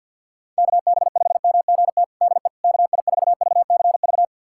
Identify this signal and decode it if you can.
Morse code, heard as a keyed tone, about 51 wpm, 700 Hz. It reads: KB5MGT DE KI4VCV
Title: USA Callsign DE Callsign - 6 Characters - 50wpm